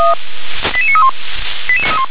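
Five-tone selective call (ZVEI), heard as quick runs of five tones, tones of about 70 ms. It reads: 88931
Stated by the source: Five-tone ZVEI selective call 88841, received on an SDR.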